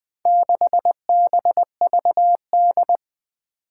Morse code, heard as a keyed tone, about 20 wpm, 700 Hz. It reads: 6BVD
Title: Four Challenging Letter-Number Combinations 20wpm